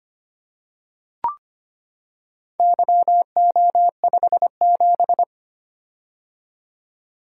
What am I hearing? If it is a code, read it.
Morse code, heard as a keyed tone, about 25 wpm, 700 Hz. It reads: YO57
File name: Four Letter-Number Combinations - 25wpm